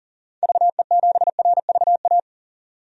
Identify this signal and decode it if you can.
Morse code, heard as a keyed tone, about 40 wpm, 700 Hz. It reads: VE7RVA
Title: DXCC-100 Call Signs - 6 Characters - 40wpm